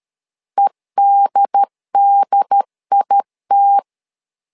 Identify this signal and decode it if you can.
Morse code, heard as a keyed tone, about 13 wpm, 800 Hz. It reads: EDDIT
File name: LEZIONI / 2 UHDN / 2 UHD 12wpm 1540